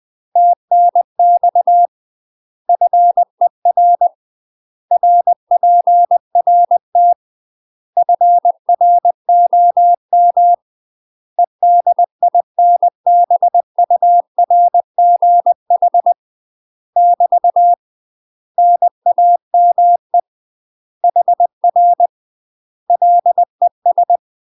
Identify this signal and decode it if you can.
Morse code, heard as a keyed tone, about 20 wpm, 700 Hz. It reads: TNX FER RPRT FROM EDINBURGH <BT> NAME HR LES
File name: UK MASTER QSOs TRAINING / 20WPM / UK MASTER QSOs 0040